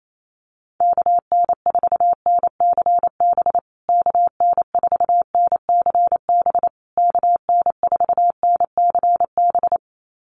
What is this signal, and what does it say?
Morse code, heard as a keyed tone, about 28 wpm, 700 Hz. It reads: KN4NCB KN4NCB KN4NCB
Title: USA Call Signs Repeated 3x 28wpm